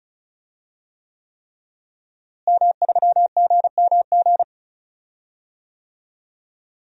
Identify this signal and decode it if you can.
Morse code, heard as a keyed tone, about 36 wpm, 700 Hz. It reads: M3GMG
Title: DXCC-100 Call Signs - 5 Characters - Repeated 3x - 35wpm